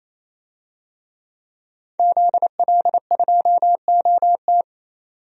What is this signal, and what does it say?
Morse code, heard as a keyed tone, about 28 wpm, 700 Hz. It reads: ZL2OT
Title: DXCC-100 Call Signs - 5 Characters - 28wpm